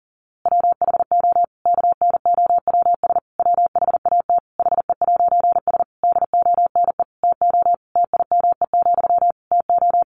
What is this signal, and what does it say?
Morse code, heard as a keyed tone, about 40 wpm, 700 Hz. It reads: WHO KNOWS WHAT HE'S DONE TO TIME, TO
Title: A Sound Of Thunder 40wpm